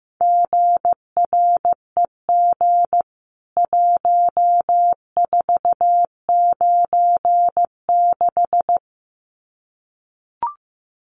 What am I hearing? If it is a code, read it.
Morse code, heard as a keyed tone, about 15 wpm, 700 Hz. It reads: GREG 1496
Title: CWT Exchange 15wpm